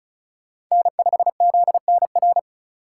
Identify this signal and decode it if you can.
Morse code, heard as a keyed tone, about 34 wpm, 700 Hz. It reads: N5ZNR
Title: USA Call Signs - 5 Characters - 3x Character Spacing - 35wpm